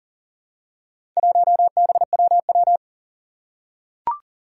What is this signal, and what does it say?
Morse code, heard as a keyed tone, about 42 wpm, 700 Hz. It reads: 1BWW
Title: Four Challenging Letter-Number Combinations 40wpm